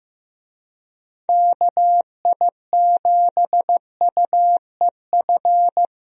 Morse code, heard as a keyed tone, about 16 wpm, 700 Hz. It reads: KI7UEF
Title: USA Call Signs - 6 Characters - Repeated 3x - 15wpm